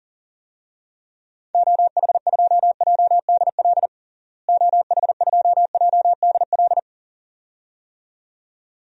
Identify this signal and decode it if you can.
Morse code, heard as a keyed tone, about 40 wpm, 700 Hz. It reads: OH2JDL OH2JDL
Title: DXCC-100 Call Signs - 6 Characters - Repeated 2x - 40wpm